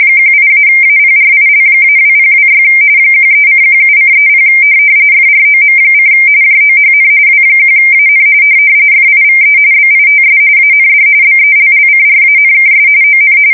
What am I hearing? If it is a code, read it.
RTTY, about 45.45 baud. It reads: 18-APRIL-2026 13:17:30 FROM: JOHN W1AN FIVE BITS AND BELLCRANKS ON
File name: fsk170 / USS-New-Jersey